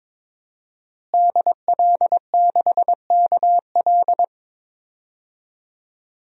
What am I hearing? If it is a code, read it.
Morse code, heard as a keyed tone, about 21 wpm, 700 Hz. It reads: DL6KL
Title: DXCC-100 Call Signs - 5 Characters - Repeated 3x - 22wpm